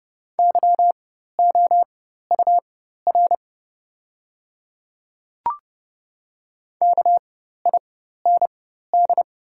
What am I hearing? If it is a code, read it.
Morse code, heard as a keyed tone, about 30 wpm, spaced as for 14 wpm, 700 Hz. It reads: YOUR KIND
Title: 4x Character Spacing - Top 200 Words - 30wpm